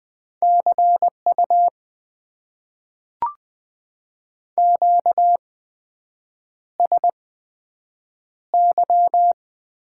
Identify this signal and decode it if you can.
Morse code, heard as a keyed tone, about 20 wpm, spaced as for 7 wpm, 700 Hz. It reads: CU QSY
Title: Top 90 QSO Elements - 8x Character Spacing - 20wpm